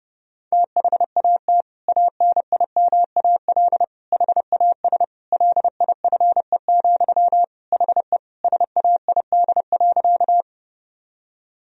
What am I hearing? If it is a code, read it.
Morse code, heard as a keyed tone, about 32 wpm, 700 Hz. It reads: THAT ANIMAL HAS LIFE, HE SAID.
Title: Sentences from Top 1000 Words - Review Part 1 - 30wpm